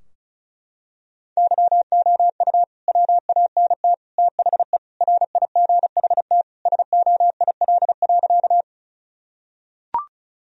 Morse code, heard as a keyed tone, about 34 wpm, 700 Hz. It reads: YOU WANT THE RIGHT SOIL.
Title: Sentences from Top 1000 Words - Review Part 4 - 35wpm